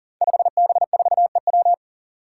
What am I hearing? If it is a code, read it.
Morse code, heard as a keyed tone, about 40 wpm, 700 Hz. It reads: 5B4EW
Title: Five Challenging Letter-Number Combinations 40wpm